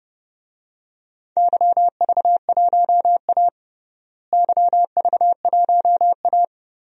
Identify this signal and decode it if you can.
Morse code, heard as a keyed tone, about 30 wpm, 700 Hz. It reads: YV1A YV1A
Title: DXCC-100 Call Signs - 4 Characters - Repeated 2x - 30wpm